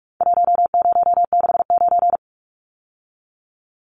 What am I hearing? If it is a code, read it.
Morse code, heard as a keyed tone, about 44 wpm, 700 Hz. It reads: 1069
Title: Numbers - Contest Style - 4-Digit 45wpm